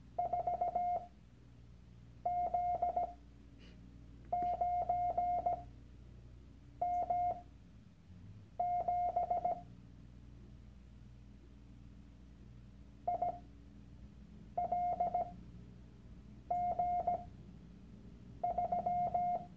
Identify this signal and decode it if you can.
Morse code, heard as a keyed tone, about 17 wpm, spaced as for 6 wpm, 710 Hz. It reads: 4Z9M7 ILG3
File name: Workshop Dateien / CW-Online-Kurs / Lektion 001